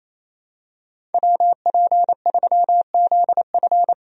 Morse code, heard as a keyed tone, about 28 wpm, 700 Hz. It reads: WP3ZF
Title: USA Call Signs - 5 Characters - 28wpm